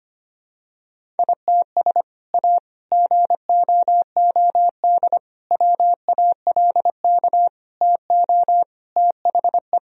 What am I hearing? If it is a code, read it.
Morse code, heard as a keyed tone, about 25 wpm, 700 Hz. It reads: ITS A GOOD WALK TO THE
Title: Sentences from Top 1000 Words - Review Part 1 - 25wpm